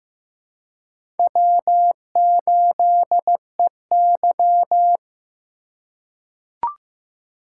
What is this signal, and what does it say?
Morse code, heard as a keyed tone, about 15 wpm, 700 Hz. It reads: W8EY
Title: USA Call Signs - 4 Characters - 4x Character Spacing - 15wpm